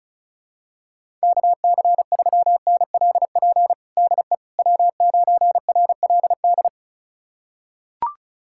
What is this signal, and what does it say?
Morse code, heard as a keyed tone, about 34 wpm, 700 Hz. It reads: KC3NLP DE W9RLD
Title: USA Callsign DE Callsign 35wpm